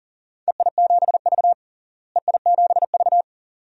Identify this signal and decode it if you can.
Morse code, heard as a keyed tone, about 40 wpm, 690 Hz. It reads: EI7V EI7V
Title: DXCC-100 Call Signs - 4 Characters - Repeated 2x - 40wpm